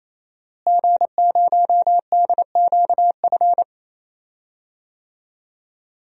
Morse code, heard as a keyed tone, about 28 wpm, 700 Hz. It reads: G0DQF
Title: DXCC-100 Call Signs - 5 Characters - Repeated 3x - 28wpm